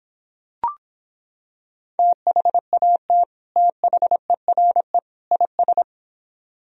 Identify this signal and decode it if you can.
Morse code, heard as a keyed tone, about 26 wpm, 700 Hz. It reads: THAT THERE IS